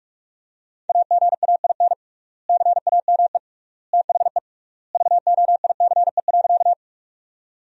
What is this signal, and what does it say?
Morse code, heard as a keyed tone, about 46 wpm, 700 Hz. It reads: AGAIN CAME THE VOICE.
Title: 3x Word-Spacing - Sentences from Top 500 Words - 45wpm